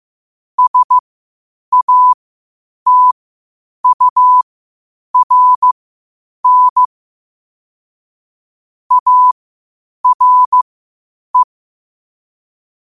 Morse code, heard as a keyed tone, about 15 wpm, spaced as for 8 wpm, 1000 Hz. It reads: SATURN ARE